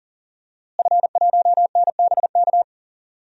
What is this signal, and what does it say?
Morse code, heard as a keyed tone, about 40 wpm, 700 Hz. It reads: F1NBK
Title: DXCC-100 Call Signs - 5 Characters - Repeated 2x - 40wpm